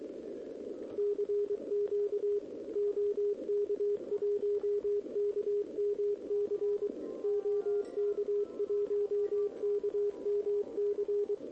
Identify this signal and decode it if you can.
Morse code, heard as a keyed tone, about 23 wpm, 410 Hz, with radio noise. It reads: CQ OK1KMC OK1KMC
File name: OK QRP 2026 / OKQRP2026